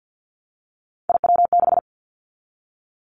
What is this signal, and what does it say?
Morse code, heard as a keyed tone, about 50 wpm, 700 Hz. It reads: IW6
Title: Three Challenging Letter-Number Combinations 50wpm